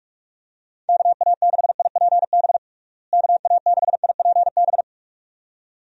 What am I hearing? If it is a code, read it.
Morse code, heard as a keyed tone, about 41 wpm, 700 Hz. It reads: KA6IPB KA6IPB
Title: USA Call Signs - 6 Characters - Repeated 2x - 45wpm